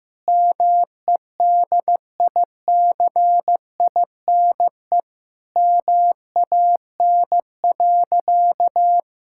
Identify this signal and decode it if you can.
Morse code, heard as a keyed tone, about 15 wpm, 700 Hz. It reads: MEDICINE MAN.
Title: Sentences from Top 1000 Words - Review Part 6 - 15wpm